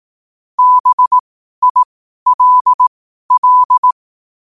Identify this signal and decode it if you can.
Morse code, heard as a keyed tone, about 18 wpm, spaced as for 12 wpm, 1000 Hz. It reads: BILL
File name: qotdfiles / 12wpm